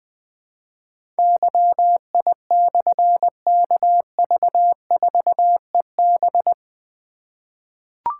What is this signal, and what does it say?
Morse code, heard as a keyed tone, about 20 wpm, 700 Hz. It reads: YI/KV4EB
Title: Difficult Call Signs 20wpm